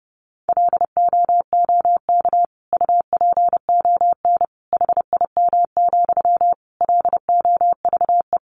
Morse code, heard as a keyed tone, about 30 wpm, 700 Hz. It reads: LOOK UPON HIM, LOVE
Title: Sentences from Top 1100 Words 30wpm